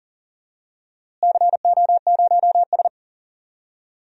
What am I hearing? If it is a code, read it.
Morse code, heard as a keyed tone, about 40 wpm, 700 Hz. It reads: CO0S
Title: DXCC-100 Call Signs - 4 Characters - Repeated 3x - 40wpm